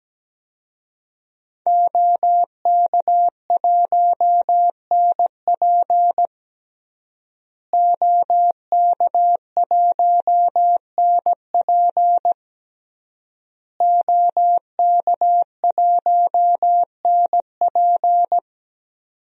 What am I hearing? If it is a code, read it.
Morse code, heard as a keyed tone, about 17 wpm, 700 Hz. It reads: OK1NP OK1NP OK1NP
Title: DXCC-100 Call Signs - 5 Characters - Repeated 3x - 17wpm